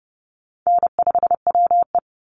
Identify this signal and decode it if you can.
Morse code, heard as a keyed tone, about 30 wpm, 700 Hz. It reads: N5WE
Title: Easy Call Signs - 30wpm